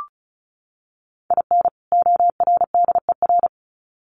Morse code, heard as a keyed tone, about 34 wpm, 700 Hz. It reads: IN ORDER
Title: Sets of 2 Words 35wpm